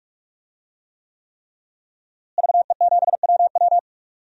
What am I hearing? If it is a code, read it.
Morse code, heard as a keyed tone, about 44 wpm, 700 Hz. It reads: VE7WW
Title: DXCC-100 Call Signs - 5 Characters - Repeated 3x - 45wpm